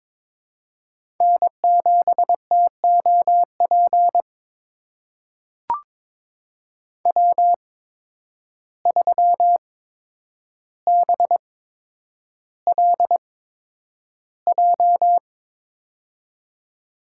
Morse code, heard as a keyed tone, about 22 wpm, spaced as for 9 wpm, 700 Hz. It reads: N7TOP W3BLJ
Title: USA Call Signs - 5 Characters - 8x Character Spacing - 22wpm